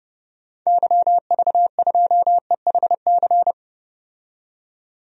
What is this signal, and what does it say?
Morse code, heard as a keyed tone, about 31 wpm, 700 Hz. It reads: YV2EHC
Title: DXCC-100 Call Signs - 6 Characters - 30wpm